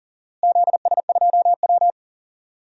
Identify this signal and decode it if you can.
Morse code, heard as a keyed tone, about 40 wpm, 700 Hz. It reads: ZS2W
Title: DXCC-100 Call Signs - 4 Characters - Repeated 3x - 40wpm